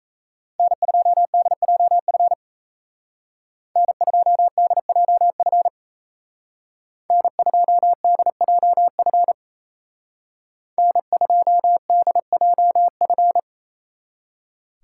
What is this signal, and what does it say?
Morse code, heard as a keyed tone, about 36 wpm, 700 Hz. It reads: N2DJF N2DJF N2DJF N2DJF